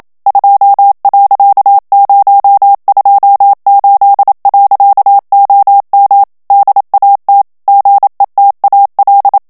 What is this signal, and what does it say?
Morse code, heard as a keyed tone, about 27 wpm, 800 Hz. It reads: 2.028. OM DAT GETAL
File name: HOLLANDAIS texte clair 1 vitesse 26 mots minute